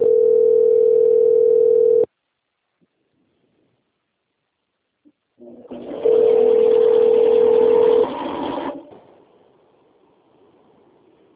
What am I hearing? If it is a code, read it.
Ringback tone (North America), 440 + 480 Hz.
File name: out- - - - - .24320